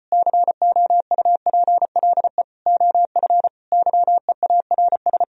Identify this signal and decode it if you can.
Morse code, heard as a keyed tone, about 33 wpm, 700 Hz. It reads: COUPLE OF YEARS